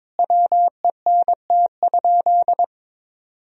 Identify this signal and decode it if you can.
Morse code, heard as a keyed tone, about 22 wpm, 700 Hz. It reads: WENT?
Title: Sentences from Top 1000 Words - Review Part 3 - 22wpm